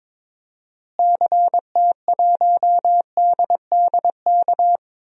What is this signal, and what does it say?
Morse code, heard as a keyed tone, about 21 wpm, 700 Hz. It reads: CT1DDK
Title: DXCC-100 Call Signs - 6 Characters - Repeated 3x - 22wpm